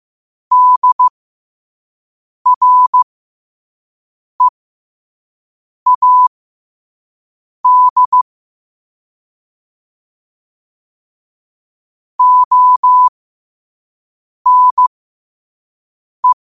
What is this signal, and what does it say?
Morse code, heard as a keyed tone, about 15 wpm, spaced as for 5 wpm, 1000 Hz. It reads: DREAD ONE
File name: qotdfiles / 5wpm